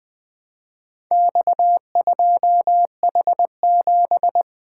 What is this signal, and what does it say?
Morse code, heard as a keyed tone, about 20 wpm, 700 Hz. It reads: X2H7
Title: Four Letter-Number Combinations - 20wpm